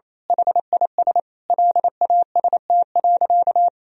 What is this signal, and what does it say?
Morse code, heard as a keyed tone, about 28 wpm, 700 Hz. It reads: HIS LAST.
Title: Sentences from Top 1000 Words - Review Part 2 - 28wpm